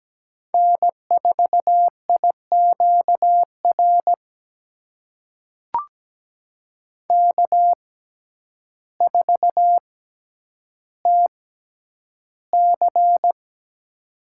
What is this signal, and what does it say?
Morse code, heard as a keyed tone, about 17 wpm, spaced as for 9 wpm, 700 Hz. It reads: N4IQR K4TC
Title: USA Call Signs - 6x Character Spacing - 17wpm